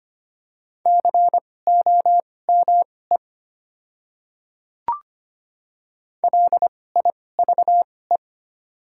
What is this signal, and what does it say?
Morse code, heard as a keyed tone, about 25 wpm, spaced as for 18 wpm, 700 Hz. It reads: COME LIVE
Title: 2x Character Spacing - Top 200 Words - 25wpm